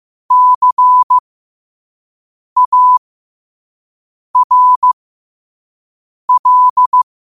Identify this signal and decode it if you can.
Morse code, heard as a keyed tone, about 15 wpm, spaced as for 5 wpm, 1000 Hz. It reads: CARL
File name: 5wpm